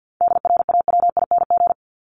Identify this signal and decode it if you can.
Morse code, heard as a keyed tone, about 48 wpm, 700 Hz. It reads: DRAWING
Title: Top 1751-2000 Words 50wpm